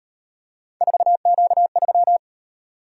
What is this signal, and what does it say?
Morse code, heard as a keyed tone, about 38 wpm, 700 Hz. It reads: 4Q3